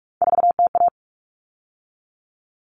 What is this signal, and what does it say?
Morse code, heard as a keyed tone, about 44 wpm, 700 Hz. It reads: 4TA
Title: All Cut Numbers - 3-Digit 45wpm